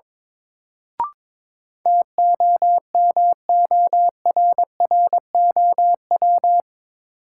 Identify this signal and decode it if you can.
Morse code, heard as a keyed tone, about 22 wpm, 700 Hz. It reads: TOMORROW
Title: Sentences from Top 1100 Words - New Word Drill Alt - 22wpm